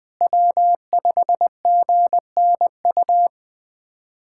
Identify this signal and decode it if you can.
Morse code, heard as a keyed tone, about 20 wpm, 700 Hz. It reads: W5GNU
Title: USA Callsign DE Callsign - 5 Characters - 20wpm